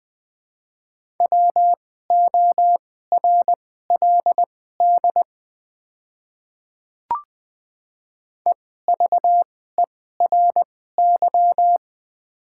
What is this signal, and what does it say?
Morse code, heard as a keyed tone, about 21 wpm, spaced as for 15 wpm, 700 Hz. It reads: WORLD EVERY
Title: 2x Character Spacing - Top 200 Words - 20wpm